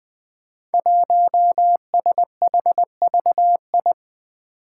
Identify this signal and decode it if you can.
Morse code, heard as a keyed tone, about 20 wpm, 700 Hz. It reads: 1SHVI
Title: Five Challenging Letter-Number Combinations 20wpm